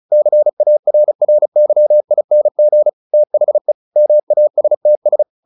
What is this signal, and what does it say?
Morse code, heard as a keyed tone, about 34 wpm, 600 Hz. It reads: CARRYING THE MASTS
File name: the old man and the sea / the old man and the sea 1 35wpm